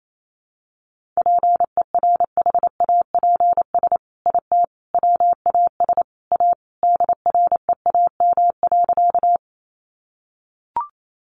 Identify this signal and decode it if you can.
Morse code, heard as a keyed tone, about 28 wpm, 700 Hz. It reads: PERHAPS IT WAS A DREAM.
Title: Sentences from Top 1500 Words 28wpm